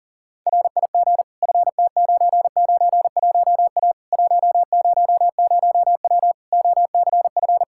Morse code, heard as a keyed tone, about 37 wpm, 700 Hz. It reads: RIG FT991A 100W OCF
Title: K1USN SST QSO Practice 40wpm